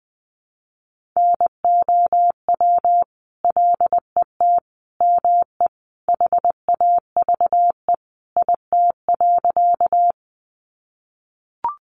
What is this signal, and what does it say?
Morse code, heard as a keyed tone, about 20 wpm, 700 Hz. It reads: NOW LET ME HAVE IT.
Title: Sentences from Top 300 Words 20wpm